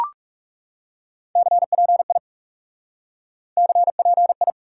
Morse code, heard as a keyed tone, about 40 wpm, 700 Hz. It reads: CPI CPI